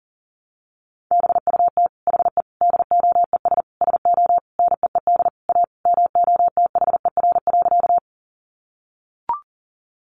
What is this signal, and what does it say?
Morse code, heard as a keyed tone, about 37 wpm, 700 Hz. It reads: BUT HE DOES SO NEED A MOTHER.
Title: Sentences from Top 200 Words 40wpm